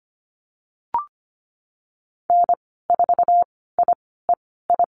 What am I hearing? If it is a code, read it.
Morse code, heard as a keyed tone, about 25 wpm, spaced as for 16 wpm, 700 Hz. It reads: N4IEI
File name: Lesson 15g - Number 5 - Call Signs 25wpm